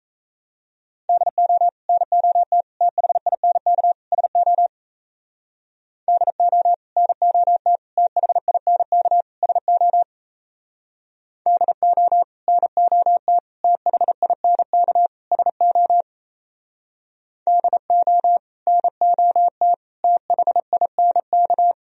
Morse code, heard as a keyed tone, about 34 wpm, 700 Hz. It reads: DO NOT THINK SO DO NOT THINK SO DO NOT THINK SO DO NOT THINK
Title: Speed-Racing - Sets of 4 Words 28wpm